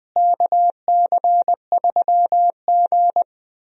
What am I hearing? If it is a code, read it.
Morse code, heard as a keyed tone, about 19 wpm, 700 Hz. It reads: KC3G
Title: USA Call Signs - 4 Characters - Repeated 3x - 20wpm